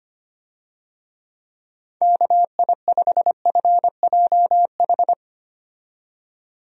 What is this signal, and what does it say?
Morse code, heard as a keyed tone, about 25 wpm, 700 Hz. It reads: KI5FJH
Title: USA Call Signs - 6 Characters - 25wpm